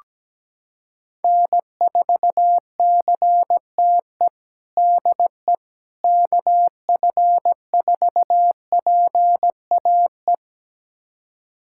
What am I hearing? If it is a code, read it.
Morse code, heard as a keyed tone, about 17 wpm, 700 Hz. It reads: N4CTE DE KF4PAE
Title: USA Callsign DE Callsign 17wpm